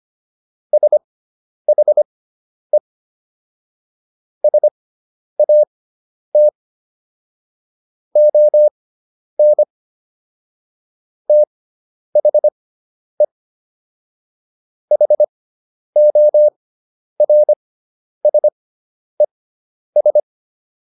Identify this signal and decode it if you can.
Morse code, heard as a keyed tone, about 25 wpm, spaced as for 10 wpm, 600 Hz. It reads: SHE SAT ON THE HORSES